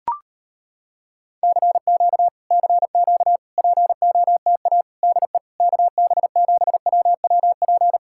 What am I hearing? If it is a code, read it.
Morse code, heard as a keyed tone, about 34 wpm, 700 Hz. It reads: CQ CQ POTA DE KB7WWP